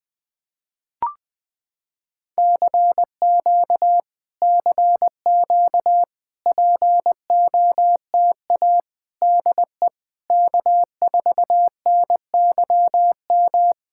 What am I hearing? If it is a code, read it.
Morse code, heard as a keyed tone, about 20 wpm, 700 Hz. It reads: CQ CQ POTA DE K4NYM
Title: POTA QSO Practice - 20wpm